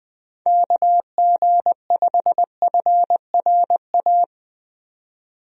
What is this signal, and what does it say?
Morse code, heard as a keyed tone, about 20 wpm, 700 Hz. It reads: KG5FRA
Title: USA Call Signs - 6 Characters - 20wpm